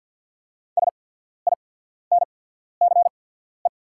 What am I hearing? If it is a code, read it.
Morse code, heard as a keyed tone, about 48 wpm, spaced as for 14 wpm, 690 Hz. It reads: SINCE
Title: 8x Character Spacing - Top 500 Words - 50wpm